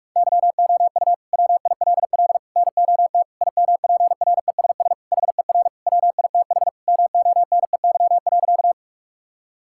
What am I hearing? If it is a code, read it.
Morse code, heard as a keyed tone, about 41 wpm, 700 Hz. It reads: YOU WILL NOT IMPRESS HER WITH MONEY.
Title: Sentences from Top 1600 Words 45wpm